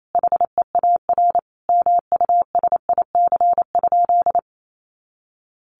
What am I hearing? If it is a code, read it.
Morse code, heard as a keyed tone, about 28 wpm, 700 Hz. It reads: HEAR MUSIC?
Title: Sentences from Top 300 Words 28wpm